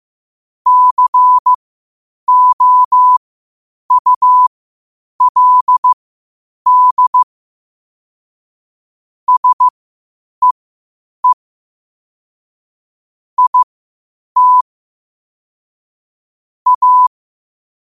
Morse code, heard as a keyed tone, about 15 wpm, spaced as for 8 wpm, 1000 Hz. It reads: COULD SEE IT A
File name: qotdfiles / 8wpm